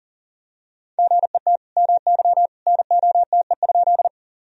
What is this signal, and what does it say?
Morse code, heard as a keyed tone, about 38 wpm, 700 Hz. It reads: GET MY NOTE?
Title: Sentences from Top 1000 Words - Review Part 2 - 40wpm